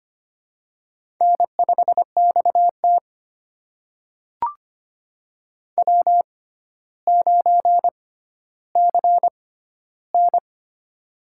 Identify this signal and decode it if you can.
Morse code, heard as a keyed tone, about 25 wpm, spaced as for 13 wpm, 700 Hz. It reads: N5XT W9CN
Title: USA Call Signs - 4 Characters - 6x Character Spacing - 25wpm